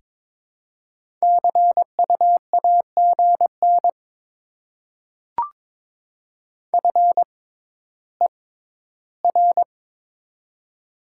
Morse code, heard as a keyed tone, about 22 wpm, spaced as for 13 wpm, 700 Hz. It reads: CU AGN FER
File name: Top 20 QSO Elements - 6x Character Spacing - 22wpm